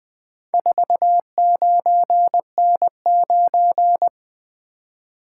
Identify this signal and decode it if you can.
Morse code, heard as a keyed tone, about 20 wpm, 700 Hz. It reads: 49N9
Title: Four Letter-Number Combinations - 20wpm